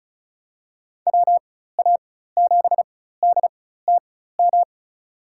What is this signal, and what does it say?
Morse code, heard as a keyed tone, about 36 wpm, spaced as for 16 wpm, 690 Hz. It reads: WA7DTM
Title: USA Call Signs - 4x Character Spacing - 35wpm